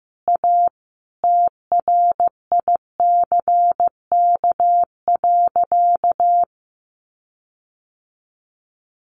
Morse code, heard as a keyed tone, about 15 wpm, 700 Hz. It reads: A TRICK.
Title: Sentences from Top 1500 Words 15wpm